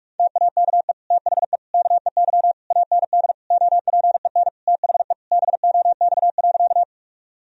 Morse code, heard as a keyed tone, about 46 wpm, 700 Hz. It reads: TAKE THE KEY AND OPEN THE BOX.
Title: Sentences from Top 1000 Words - Review Part 4 - 45wpm